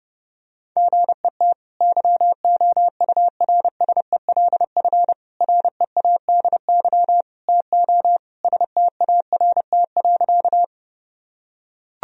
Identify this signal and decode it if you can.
Morse code, heard as a keyed tone, about 30 wpm, 700 Hz. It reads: GET YOURSELF READY TO START.
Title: Sentences from Top 1000 Words - Review Part 1 - 30wpm